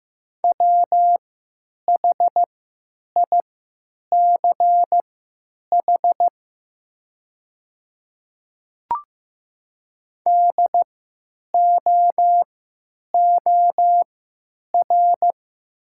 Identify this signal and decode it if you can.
Morse code, heard as a keyed tone, about 15 wpm, spaced as for 9 wpm, 700 Hz. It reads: WHICH DOOR
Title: 3x Character Spacing - Top 300 Words - 15wpm